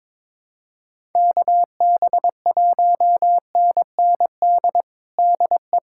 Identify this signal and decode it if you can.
Morse code, heard as a keyed tone, about 22 wpm, 700 Hz. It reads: KB1NND DE
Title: USA Callsign DE Callsign - 6 Characters - 22wpm